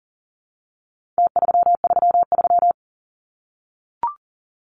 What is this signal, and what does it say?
Morse code, heard as a keyed tone, about 40 wpm, 700 Hz. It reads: T333
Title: Numbers - Contest Style - 4-Digit 40wpm